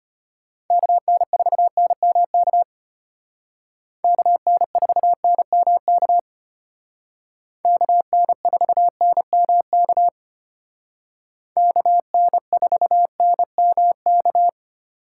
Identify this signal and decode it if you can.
Morse code, heard as a keyed tone, about 32 wpm, 700 Hz. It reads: KN4NMK KN4NMK KN4NMK KN4NMK